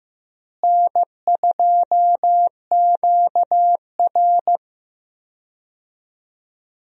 Morse code, heard as a keyed tone, about 15 wpm, 700 Hz. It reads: N2QR
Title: USA Call Signs - 4 Characters - Repeated 3x - 15wpm